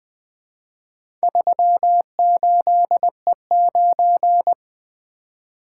Morse code, heard as a keyed tone, about 20 wpm, 700 Hz. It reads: 38E9
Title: Four Letter-Number Combinations - 20wpm